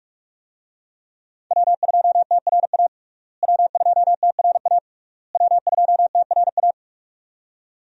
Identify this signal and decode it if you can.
Morse code, heard as a keyed tone, about 46 wpm, 700 Hz. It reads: W2TRA W2TRA W2TRA
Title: USA Call Signs - 5 Characters - Repeated 3x - 45wpm